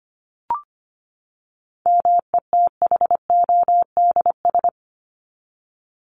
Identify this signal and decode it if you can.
Morse code, heard as a keyed tone, about 25 wpm, 700 Hz. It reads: METHODS
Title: ES:S Suffix - Top 100 Words with Base Words 25wpm